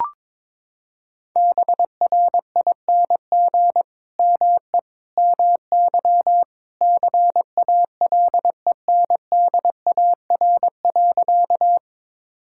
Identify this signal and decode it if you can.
Morse code, heard as a keyed tone, about 22 wpm, 700 Hz. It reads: BRING ME MY CALENDAR.
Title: Sentences from Top 1600 Words 22wpm